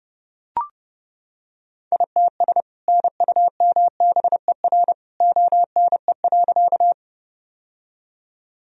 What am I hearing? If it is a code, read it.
Morse code, heard as a keyed tone, about 32 wpm, 700 Hz. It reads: ITS NUMBER ONE.
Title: Sentences from Top 1000 Words - Review Part 1 - 30wpm